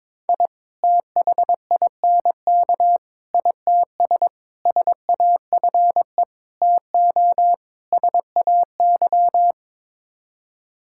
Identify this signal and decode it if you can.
Morse code, heard as a keyed tone, about 22 wpm, 700 Hz. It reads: I THINK ITS SAFE TO SAY
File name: practice / 22wpm / Sets of 6 Words - 22wpm